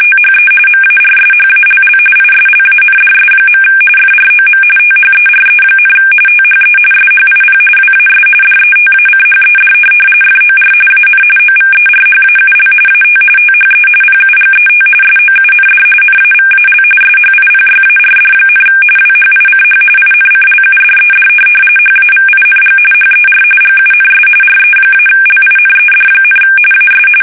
RTTY, about 45.45 baud. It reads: THE MARSHALL ISLANDS. FROM THERE, THE SHIP WAS DETACHED FOR AN OVERHAUL AT THE PUGET SOUND NAVAL SHIPYARD, WHICH INCLUDED HAVING HER GUNS RE-LINED, AS
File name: fsk850 / USS-Massachusetts